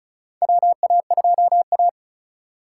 Morse code, heard as a keyed tone, about 36 wpm, 700 Hz. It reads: WA2A
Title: USA Call Signs - 3x Character Spacing - 35wpm